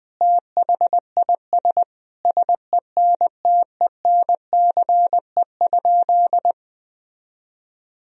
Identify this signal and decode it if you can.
Morse code, heard as a keyed tone, about 21 wpm, 700 Hz. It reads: THIS SENTENCE?
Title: Sentences from Top 1000 Words - Review Part 1 - 20wpm